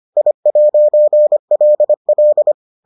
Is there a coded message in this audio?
Yes: Morse code from 0.2 to 2.5 s, heard as a keyed tone, about 25 wpm, 600 Hz. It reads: I'LL